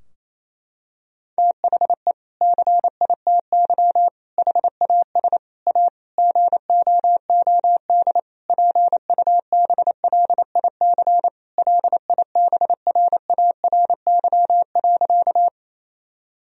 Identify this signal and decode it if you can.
Morse code, heard as a keyed tone, about 28 wpm, 700 Hz. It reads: THE CITY HAS A GOOD PUBLIC LIBRARY.
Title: Sentences from Top 1000 Words - Review Part 4 - 28wpm